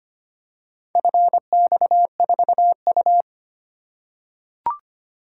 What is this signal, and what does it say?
Morse code, heard as a keyed tone, about 25 wpm, 700 Hz. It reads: FX4U
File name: practice / 25wpm / Four Letter-Number Combinations - 25wpm